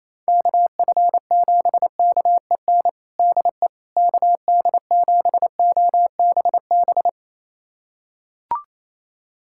Morse code, heard as a keyed tone, about 28 wpm, 700 Hz. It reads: KF7KEN DE KD7OBB
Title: USA Callsign DE Callsign - 6 Characters - 28wpm